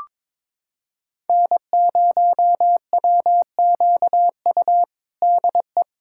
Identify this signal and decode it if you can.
Morse code, heard as a keyed tone, about 21 wpm, 700 Hz. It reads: N0WQU DE
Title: USA Callsign DE Callsign 22wpm